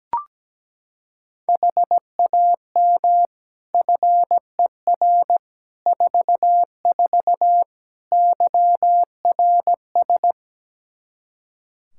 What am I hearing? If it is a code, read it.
Morse code, heard as a keyed tone, about 17 wpm, 700 Hz. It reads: HAM FER 44 YRS
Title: K1USN SST QSO Practice 17wpm